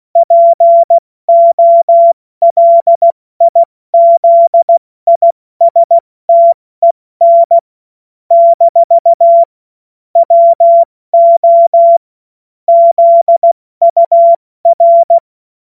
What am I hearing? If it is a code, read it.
Morse code, heard as a keyed tone, about 16 wpm, 680 Hz. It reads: POLIZISTEN - WO ZUR
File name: Morsekurs / Witze / Witz 02. August 2022 70 F80